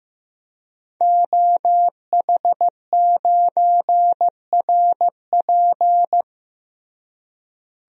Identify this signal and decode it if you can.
Morse code, heard as a keyed tone, about 15 wpm, 700 Hz. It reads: OH9RP
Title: DXCC-100 Call Signs - 5 Characters - Repeated 2x - 15wpm